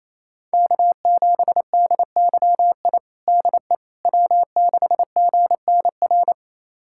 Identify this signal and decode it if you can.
Morse code, heard as a keyed tone, about 28 wpm, 700 Hz. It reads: K7DYI DE W6GNR
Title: USA Callsign DE Callsign - 5 Characters - 28wpm